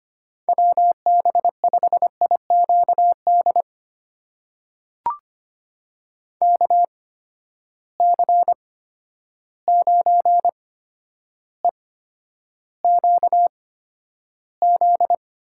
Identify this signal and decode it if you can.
Morse code, heard as a keyed tone, about 25 wpm, spaced as for 11 wpm, 700 Hz. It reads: WB5IQD KC9EQZ
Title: USA Call Signs - 6 Characters - 8x Character Spacing - 25wpm